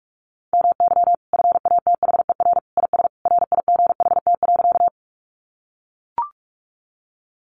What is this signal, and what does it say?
Morse code, heard as a keyed tone, about 44 wpm, 700 Hz. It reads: MY FATHER IS RIGHT.
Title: Sentences from Top 300 Words 45wpm